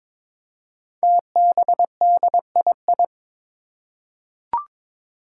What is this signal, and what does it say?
Morse code, heard as a keyed tone, about 22 wpm, 700 Hz. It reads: TBDII
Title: Five Challenging Letter-Number Combinations 22wpm